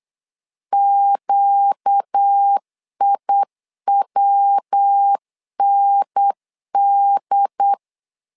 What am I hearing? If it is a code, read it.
Morse code, heard as a keyed tone, about 8 wpm, 800 Hz. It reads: QIWND
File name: LEZIONI / 4 Q 8wpm 2732